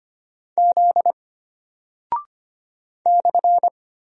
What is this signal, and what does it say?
Morse code, heard as a keyed tone, about 25 wpm, 700 Hz. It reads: Z /
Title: Single Character (Letter:Number:Punctuation:Prosign) Alt 25wpm